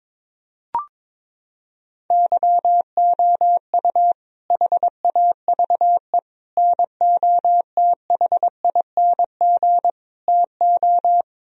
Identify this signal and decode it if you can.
Morse code, heard as a keyed tone, about 21 wpm, 700 Hz. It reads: YOU HAVE NOTHING TO
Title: Sentences from Top 1000 Words - Review Part 1 - 22wpm